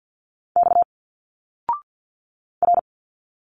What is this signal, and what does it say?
Morse code, heard as a keyed tone, about 50 wpm, 700 Hz. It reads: X R
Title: Single Letters 50wpm